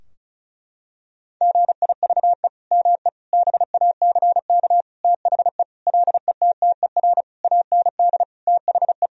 Morse code, heard as a keyed tone, about 36 wpm, 700 Hz. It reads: GIVE ME BACK THE LETTER AND THE
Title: Sentences from Top 1000 Words - Review Part 4 - 35wpm